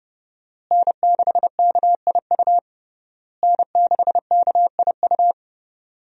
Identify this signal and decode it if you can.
Morse code, heard as a keyed tone, about 32 wpm, 700 Hz. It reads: N6KIU N6KIU
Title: USA Call Signs - 5 Characters - Repeated 3x - 30wpm